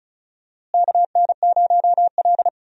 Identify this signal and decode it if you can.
Morse code, heard as a keyed tone, about 34 wpm, 700 Hz. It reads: KN0L
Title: USA Call Signs - 4 Characters - Repeated 3x - 35wpm